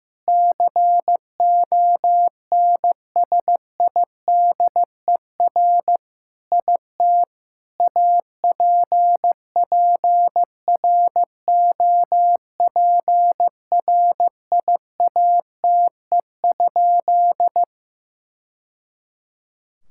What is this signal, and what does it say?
Morse code, heard as a keyed tone, about 15 wpm, 700 Hz. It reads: CONSIDER IT APPROPRIATE?
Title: Sentences from Top 1000 Words - Review Part 5 - 15wpm